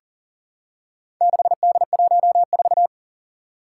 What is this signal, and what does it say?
Morse code, heard as a keyed tone, about 40 wpm, 700 Hz. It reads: 6D14
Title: Four Challenging Letter-Number Combinations 40wpm